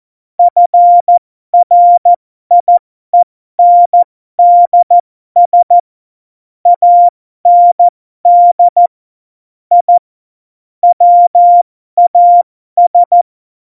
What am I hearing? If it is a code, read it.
Morse code, heard as a keyed tone, about 14 wpm, spaced as for 12 wpm, 700 Hz. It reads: FRIENDS AND I WAS
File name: Morsekurs / Witze